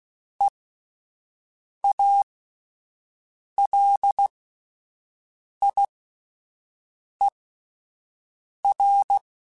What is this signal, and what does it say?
Morse code, heard as a keyed tone, about 16 wpm, spaced as for 5 wpm, 800 Hz. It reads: EALIER
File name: Spaced Characters / Spaced Characters-mp3s / Weather5 5 sp @10